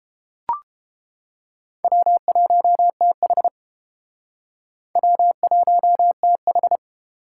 Morse code, heard as a keyed tone, about 29 wpm, 700 Hz. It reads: W1TH W1TH